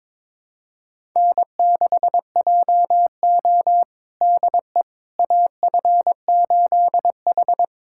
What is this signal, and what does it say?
Morse code, heard as a keyed tone, about 22 wpm, 700 Hz. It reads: N6JO DE AF8H
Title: USA Callsign DE Callsign - 4 Characters - 22wpm